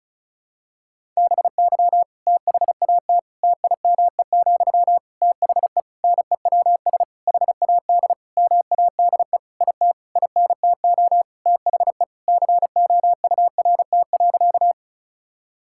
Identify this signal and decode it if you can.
Morse code, heard as a keyed tone, about 34 wpm, 700 Hz. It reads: BY THAT TIME, THE NEWS HAD MADE IT INTO THE COURT.
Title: Sentences from Top 1600 Words 35wpm